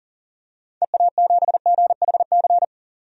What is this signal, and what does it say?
Morse code, heard as a keyed tone, about 40 wpm, 700 Hz. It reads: EA7GHC
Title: DXCC-100 Call Signs - 6 Characters - 40wpm